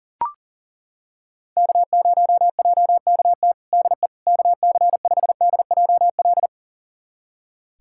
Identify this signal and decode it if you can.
Morse code, heard as a keyed tone, about 40 wpm, 700 Hz. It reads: K0JKT DE KC5DJL
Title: USA Callsign DE Callsign 40wpm